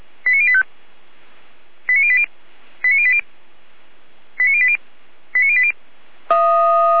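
Five-tone selective call (ZVEI), heard as quick runs of five tones, tones of about 70 ms. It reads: 89085 89080 89080 89080 89080
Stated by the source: ZVEI five-tone selective call, code 89115, received by SDR.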